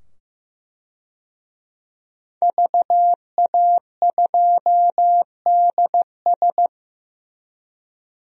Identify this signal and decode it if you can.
Morse code, heard as a keyed tone, about 15 wpm, 700 Hz. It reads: VA2DS
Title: DXCC-100 Call Signs - 5 Characters - 15wpm